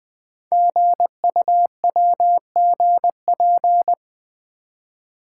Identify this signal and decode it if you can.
Morse code, heard as a keyed tone, about 20 wpm, 700 Hz. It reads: GUWGP
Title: Code Groups - 5 Letters - 20wpm